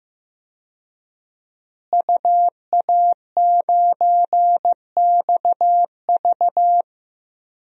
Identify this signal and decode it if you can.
Morse code, heard as a keyed tone, about 15 wpm, 700 Hz. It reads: UA9XV
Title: DXCC-100 Call Signs - 5 Characters - Repeated 3x - 15wpm